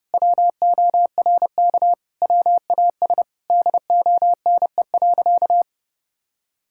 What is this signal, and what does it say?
Morse code, heard as a keyed tone, about 30 wpm, 700 Hz. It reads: WORK WAS DONE.
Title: Sentences from Top 1000 Words - Review Part 5 - 30wpm